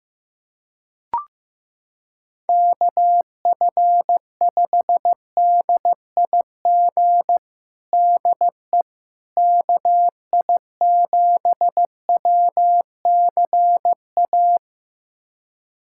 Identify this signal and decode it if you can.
Morse code, heard as a keyed tone, about 15 wpm, 700 Hz. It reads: KF5DIG DE KI7WCA
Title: USA Callsign DE Callsign 15wpm